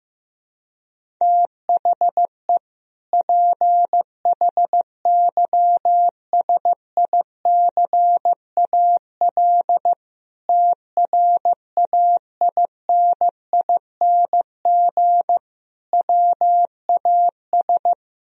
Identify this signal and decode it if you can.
Morse code, heard as a keyed tone, about 15 wpm, 700 Hz. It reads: THE PHYSICAL TRAINING WAS
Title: Sentences from Top 1000 Words - Review Part 4 - 15wpm